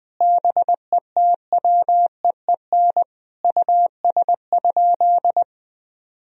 Morse code, heard as a keyed tone, about 21 wpm, 700 Hz. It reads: BETWEEN US?
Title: Sentences from Top 1000 Words - Review Part 1 - 20wpm